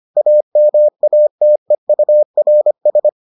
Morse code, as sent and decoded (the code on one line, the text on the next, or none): .- -- .- - . ..- .-. ...
AMATEURS